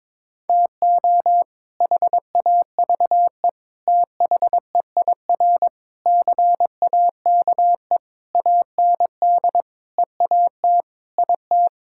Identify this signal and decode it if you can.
Morse code, heard as a keyed tone, about 22 wpm, 700 Hz. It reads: TO HAVE THEIR CAKE AND EAT IT
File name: practice / 22wpm / Sets of 8 Words - 22wpm